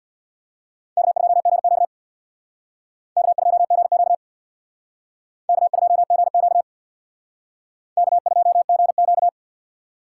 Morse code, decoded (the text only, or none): K2GQ K2GQ K2GQ K2GQ